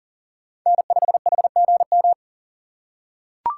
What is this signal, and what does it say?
Morse code, heard as a keyed tone, about 40 wpm, 700 Hz. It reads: N5HGM
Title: USA Call Signs - 5 Characters - 3x Character Spacing - 40wpm